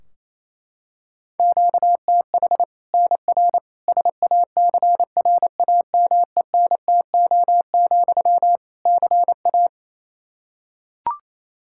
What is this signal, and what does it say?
Morse code, heard as a keyed tone, about 28 wpm, 700 Hz. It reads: QTH NR SACRAMENTO, CA
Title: Practice QSO - Exchange 1 - Beginning - 28wpm